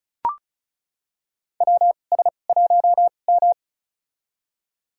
Morse code, heard as a keyed tone, about 34 wpm, spaced as for 25 wpm, 700 Hz. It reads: WS1M
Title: USA Call Signs - 2x Character Spacing - 35wpm